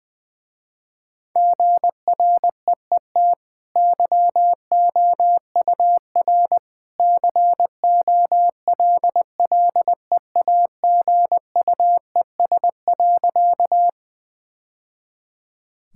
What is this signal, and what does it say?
Morse code, heard as a keyed tone, about 20 wpm, 700 Hz. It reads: GREET YOUR COLLEAGUES.
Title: Easy Sentences with Hints - 20wpm